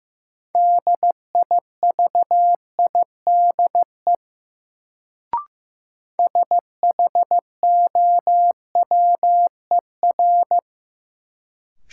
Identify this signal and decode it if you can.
Morse code, heard as a keyed tone, about 15 wpm, 700 Hz. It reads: DIVIDE SHOWER
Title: Sentences from Top 1600 Words - New Word Drill - 15wpm